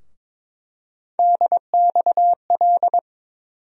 Morse code, heard as a keyed tone, about 22 wpm, 700 Hz. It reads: DXL
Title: USA Call Sign Postfixes - 3 Letters 22wpm